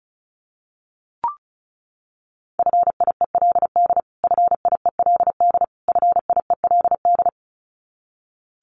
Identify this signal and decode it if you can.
Morse code, heard as a keyed tone, about 34 wpm, 700 Hz. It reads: FIELD FIELD FIELD
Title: Top 500 Words Repeated 35wpm